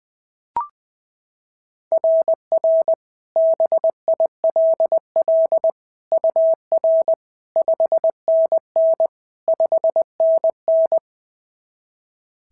Morse code, heard as a keyed tone, about 21 wpm, 660 Hz. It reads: RR BILL UR 5NN 5NN
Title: SOTA QSO Practice 20wpm